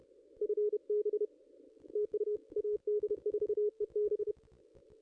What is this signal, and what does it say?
Morse code, heard as a keyed tone, about 31 wpm, 410 Hz. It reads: FB TU AD4EB
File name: contests / ss-cw-k5tr-op-n5rz -0500